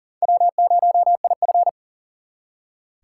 Morse code, heard as a keyed tone, about 50 wpm, 700 Hz. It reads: W0IF
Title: Speed Racing - USA Call Signs 40wpm